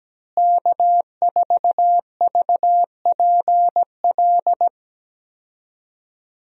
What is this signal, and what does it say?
Morse code, heard as a keyed tone, about 17 wpm, 700 Hz. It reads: K4VPL
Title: USA Call Signs - 5 Characters - Repeated 3x - 17wpm